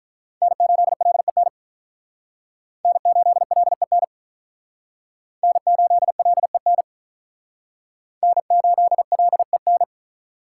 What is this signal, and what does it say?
Morse code, heard as a keyed tone, about 45 wpm, 700 Hz. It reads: N8LEN N8LEN N8LEN N8LEN